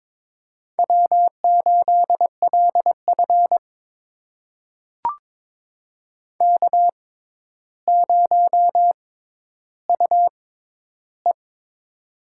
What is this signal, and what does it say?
Morse code, heard as a keyed tone, about 22 wpm, spaced as for 11 wpm, 700 Hz. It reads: W8LF K0UE
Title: USA Call Signs - 4 Characters - 6x Character Spacing - 22wpm